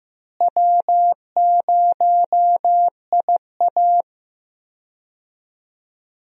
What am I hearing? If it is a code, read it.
Morse code, heard as a keyed tone, about 16 wpm, 700 Hz. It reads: W0IA
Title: USA Call Signs - 4 Characters - Repeated 3x - 15wpm